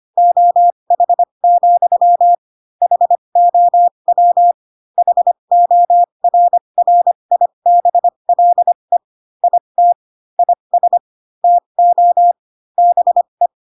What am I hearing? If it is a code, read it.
Morse code, heard as a keyed tone, about 25 wpm, spaced as for 22 wpm, 700 Hz. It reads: OH, HOW HORRIBLE IT IS TO BE